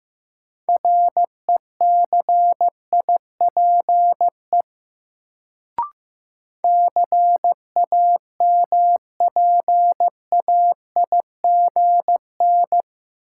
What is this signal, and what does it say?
Morse code, heard as a keyed tone, about 15 wpm, 700 Hz. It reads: RECIPE CAMPAIGN
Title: Sentences from Top 1200 Words - New Word Drill - 15wpm